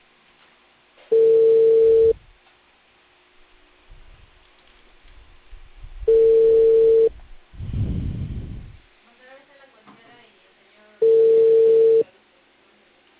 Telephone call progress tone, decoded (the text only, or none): ringback tone